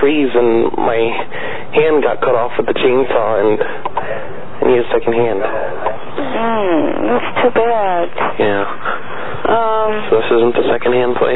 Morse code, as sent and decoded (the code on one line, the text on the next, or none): none